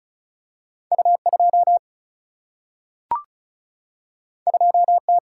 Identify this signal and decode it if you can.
Morse code, heard as a keyed tone, about 36 wpm, 700 Hz. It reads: U2 2T
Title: Single-Letter Single-Number 35wpm